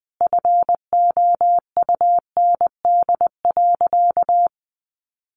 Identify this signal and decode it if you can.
Morse code, heard as a keyed tone, about 20 wpm, 700 Hz. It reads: FOUND.
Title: Sentences from Top 1500 Words 20wpm